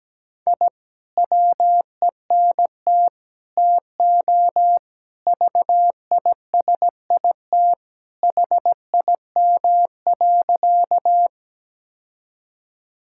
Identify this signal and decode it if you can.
Morse code, heard as a keyed tone, about 17 wpm, 700 Hz. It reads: I WENT TO VISIT HIM.
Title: Sentences from Top 1000 Words - Review Part 6 - 17wpm